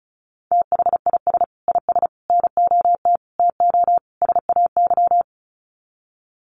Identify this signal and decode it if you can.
Morse code, heard as a keyed tone, about 34 wpm, 700 Hz. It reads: THIS IS NOT TO SAY
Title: Sets of 5 Words Encore 35wpm